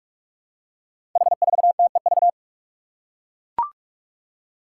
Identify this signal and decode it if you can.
Morse code, heard as a keyed tone, about 44 wpm, 700 Hz. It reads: H4TEV
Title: Five Challenging Letter-Number Combinations 45wpm